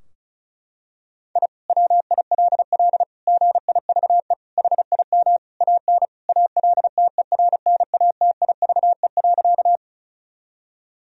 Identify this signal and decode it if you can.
Morse code, heard as a keyed tone, about 36 wpm, 700 Hz. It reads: I WILL GIVE HIM AN ALTERNATIVE.
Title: Sentences from Top 1000 Words - Review Part 4 - 35wpm